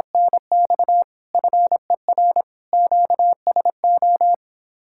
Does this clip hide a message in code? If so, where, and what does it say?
Yes: Morse code from 0.1 to 4.3 s, heard as a keyed tone, about 26 wpm, 700 Hz. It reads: NX FER QSO